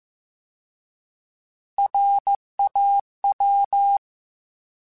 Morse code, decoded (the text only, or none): RAW